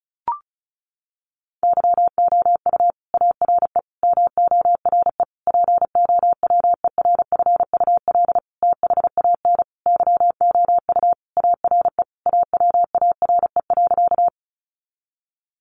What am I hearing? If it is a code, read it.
Morse code, heard as a keyed tone, about 34 wpm, 700 Hz. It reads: YOU ARE MORE POWERFUL THAN YOU ARE AWARE.
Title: Sentences from Top 1300 Words 35wpm